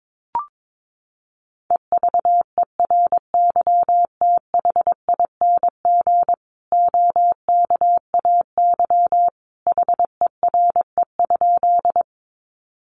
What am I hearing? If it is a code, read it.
Morse code, heard as a keyed tone, about 22 wpm, 700 Hz. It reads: EVERYTHING OKAY HERE?
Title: A Sound Of Thunder 22wpm